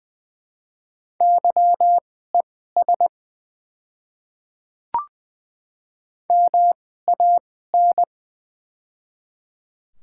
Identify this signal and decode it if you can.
Morse code, heard as a keyed tone, about 20 wpm, spaced as for 15 wpm, 690 Hz. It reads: YES MAN
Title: 2x Character Spacing - Top 500 Words - 20wpm